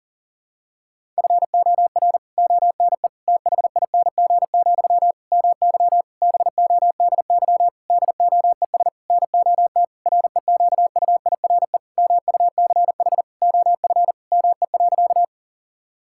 Morse code, heard as a keyed tone, about 40 wpm, 700 Hz. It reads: FOR ONE THING, MY BODY DOES NOT REQUIRE MUCH OF ME.
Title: Sentences from Top 1000 Words - Review Part 6 - 40wpm